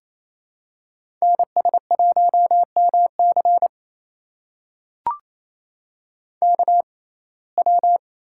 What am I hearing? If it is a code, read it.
Morse code, heard as a keyed tone, about 28 wpm, spaced as for 20 wpm, 700 Hz. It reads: NS1MC KW
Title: USA Call Signs - 5 Characters - 6x Character Spacing - 28wpm